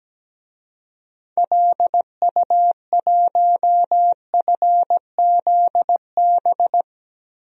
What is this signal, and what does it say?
Morse code, heard as a keyed tone, about 17 wpm, 700 Hz. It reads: LU1FZB
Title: DXCC-100 Call Signs - 6 Characters - 17wpm